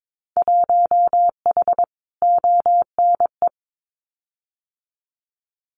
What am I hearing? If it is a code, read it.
Morse code, heard as a keyed tone, about 22 wpm, 700 Hz. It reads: 1H ONE
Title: Winter Field Day Exchanges 22wpm